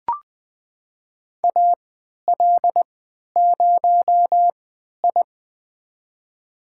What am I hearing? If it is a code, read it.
Morse code, heard as a keyed tone, about 20 wpm, spaced as for 11 wpm, 700 Hz. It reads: AL0I
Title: USA Call Signs - 4 Characters - 3x Character Spacing - 20wpm